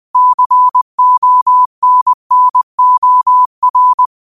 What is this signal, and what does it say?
Morse code, heard as a keyed tone, about 20 wpm, 1000 Hz. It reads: CONNOR